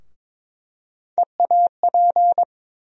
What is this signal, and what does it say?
Morse code, heard as a keyed tone, about 21 wpm, 700 Hz. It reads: EAP
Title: USA Call Sign Postfixes - 3 Letters 22wpm